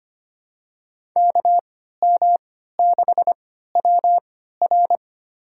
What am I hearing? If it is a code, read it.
Morse code, heard as a keyed tone, about 25 wpm, spaced as for 14 wpm, 700 Hz. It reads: KM6WR
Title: USA Call Signs - 3x Character Spacing - 25wpm